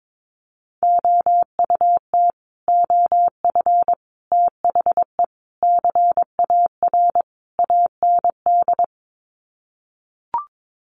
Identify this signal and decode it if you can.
Morse code, heard as a keyed tone, about 22 wpm, 700 Hz. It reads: OUT OF THE CAR AND
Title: Sets of 5 Words Encore 22wpm